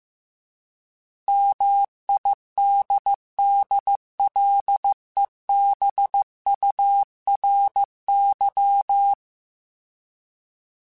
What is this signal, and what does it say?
Morse code, heard as a keyed tone, about 15 wpm, 800 Hz. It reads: MIDDLEBURY